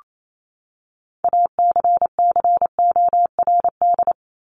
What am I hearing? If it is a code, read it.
Morse code, heard as a keyed tone, about 28 wpm, 700 Hz. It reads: ACCORD
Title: ING Suffix - Top 100 Words with Base Words 28wpm